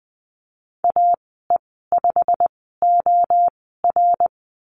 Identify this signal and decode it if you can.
Morse code, heard as a keyed tone, about 20 wpm, spaced as for 15 wpm, 700 Hz. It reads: AE5OR
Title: 186 Lesson 30g - Letter b - Call Signs 20wpm